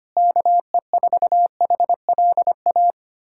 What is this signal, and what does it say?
Morse code, heard as a keyed tone, about 25 wpm, 700 Hz. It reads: KE4HLA
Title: USA Callsign DE Callsign - 6 Characters - 25wpm